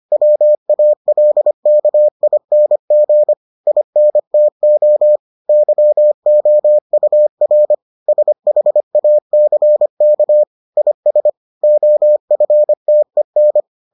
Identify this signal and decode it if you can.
Morse code, heard as a keyed tone, about 25 wpm, 600 Hz. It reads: WALKING INTO YOUR SHACK IS OFTEN